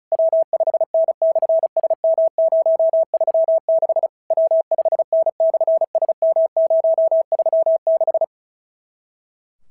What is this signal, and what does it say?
Morse code, heard as a keyed tone, about 34 wpm, 650 Hz. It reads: W5N/SM036 W5N/SM036
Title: SOTA QSO Practice 35wpm